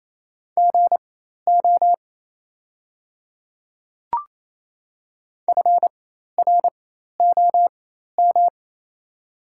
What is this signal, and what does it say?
Morse code, heard as a keyed tone, about 27 wpm, spaced as for 13 wpm, 700 Hz. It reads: GO FROM
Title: 4x Character Spacing - Top 200 Words - 28wpm